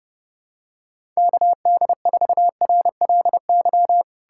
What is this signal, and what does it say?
Morse code, heard as a keyed tone, about 30 wpm, 700 Hz. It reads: KD4RLY
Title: USA Callsign DE Callsign 30wpm